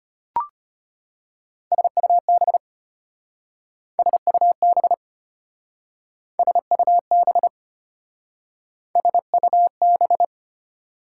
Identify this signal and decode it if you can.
Morse code, heard as a keyed tone, about 32 wpm, 700 Hz. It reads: SUB SUB SUB SUB